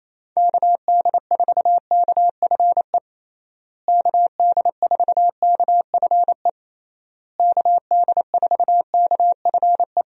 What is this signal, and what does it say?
Morse code, heard as a keyed tone, about 28 wpm, 700 Hz. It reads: KD4KFE KD4KFE KD4KFE
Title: USA Call Signs - 6 Characters - Repeated 3x - 28wpm